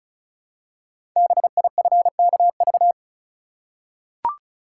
Morse code, heard as a keyed tone, about 36 wpm, 690 Hz. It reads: BIFKV
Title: Code Groups - 5 Letter:Numbers - 35wpm